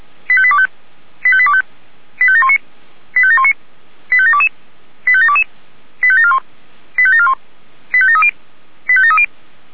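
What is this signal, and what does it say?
Five-tone selective call (ZVEI), heard as quick runs of five tones, tones of about 70 ms. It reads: 85825 85825 85719 85719 85722 85722 85741 85741 85730 85730
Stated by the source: ZVEI five-tone selective call received via SDR.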